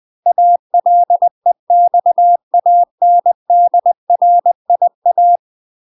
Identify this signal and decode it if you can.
Morse code, heard as a keyed tone, about 20 wpm, 700 Hz. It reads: ALEXANDRIA
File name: UK MASTER QSOs TRAINING / 20WPM / UK MASTER QSOs 0011